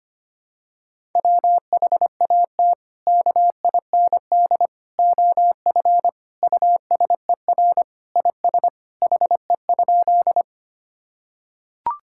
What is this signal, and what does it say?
Morse code, heard as a keyed tone, about 25 wpm, 700 Hz. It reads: WHAT KIND OF USER IS HE?
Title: Sentences from Top 1000 Words - Review Part 5 - 25wpm